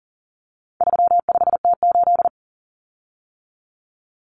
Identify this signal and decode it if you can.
Morse code, heard as a keyed tone, about 40 wpm, 700 Hz. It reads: 35T8
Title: Common Cut Numbers - 4-Digit 40wpm